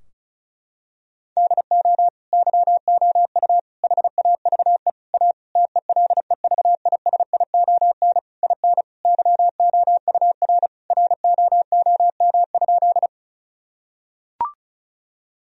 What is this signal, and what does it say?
Morse code, heard as a keyed tone, about 36 wpm, 700 Hz. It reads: DO YOU HAVE A TELEVISION IN YOUR ROOM?
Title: Sentences from Top 1000 Words - Review Part 4 - 35wpm